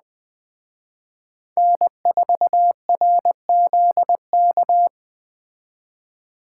Morse code, heard as a keyed tone, about 20 wpm, 700 Hz. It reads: N4RZK
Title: Speed Racing - USA 1x3 Call Signs 17wpm